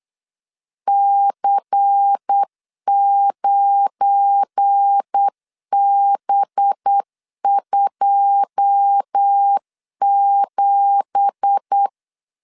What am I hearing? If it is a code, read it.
Morse code, heard as a keyed tone, about 8 wpm, 800 Hz. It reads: C9B27
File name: LEZIONI / 9 2867 8wpm 3440